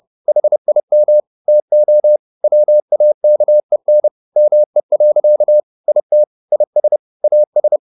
Morse code, heard as a keyed tone, about 30 wpm, 600 Hz. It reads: HIM TO WAKEN ME. IT IS AS
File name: the old man and the sea / the old man and the sea 1 30wpm